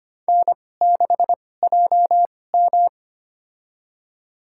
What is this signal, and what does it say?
Morse code, heard as a keyed tone, about 25 wpm, spaced as for 18 wpm, 700 Hz. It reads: N6JM
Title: USA Call Signs - 2x Character Spacing - 25wpm